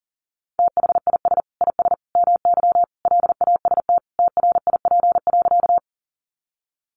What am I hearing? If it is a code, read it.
Morse code, heard as a keyed tone, about 40 wpm, 700 Hz. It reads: THIS IS MY LAST TRIP.
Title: Sentences from Top 1300 Words 40wpm